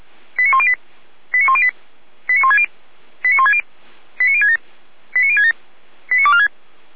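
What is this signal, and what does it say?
Five-tone selective call (ZVEI), heard as quick runs of five tones, tones of about 70 ms. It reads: 89108 89108 89160 89160 89067 89067 89246
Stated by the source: ZVEI five-tone call received by SDR.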